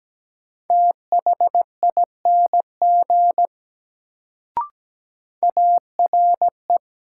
Morse code, heard as a keyed tone, about 17 wpm, 700 Hz. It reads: THING ARE
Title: Sentences from Top 100 Words - New Word Drill - 17wpm